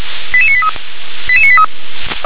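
Five-tone selective call (ZVEI), heard as quick runs of five tones, tones of about 70 ms. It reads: 88983 88983
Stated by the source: ZVEI five-tone selective call 88705, received by SDR.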